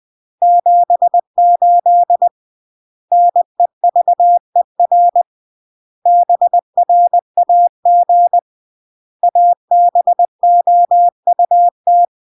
Morse code, heard as a keyed tone, about 20 wpm, 700 Hz. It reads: 78 NEVER BRAG ABOUT
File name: CW / jodorowsky / 82 commandments 20-20